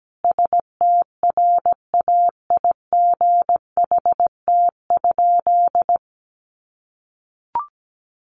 Morse code, heard as a keyed tone, about 17 wpm, 700 Hz. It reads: STRAIGHT?
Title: Sentences from Top 1000 Words 17wpm